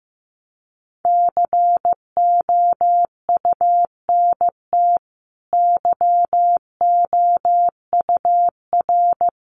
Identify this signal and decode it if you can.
Morse code, heard as a keyed tone, about 15 wpm, 700 Hz. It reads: COUNT YOUR
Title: Sentences from Top 1500 Words 15wpm